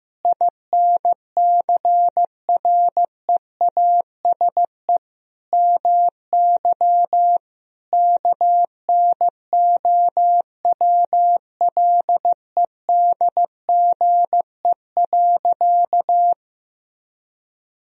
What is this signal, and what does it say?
Morse code, heard as a keyed tone, about 15 wpm, 700 Hz. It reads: INCREASE MY KNOWLEDGE.
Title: Sentences from Top 1000 Words - Review Part 4 - 15wpm